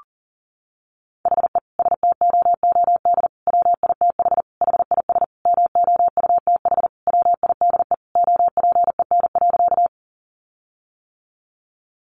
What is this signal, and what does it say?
Morse code, heard as a keyed tone, about 40 wpm, 700 Hz. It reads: HE STOOD WITH HIS MOUTH WIDE OPEN.
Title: Sentences from Top 1500 Words 40wpm